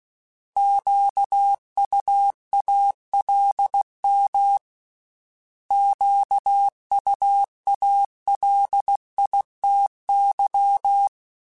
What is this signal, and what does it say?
Morse code, heard as a keyed tone, about 16 wpm, 800 Hz. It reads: QUALM QUALITY
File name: Misc / Misc-mp3s / Misc @15